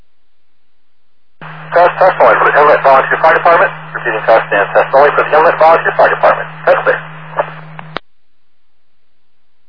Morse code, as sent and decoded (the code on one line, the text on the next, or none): none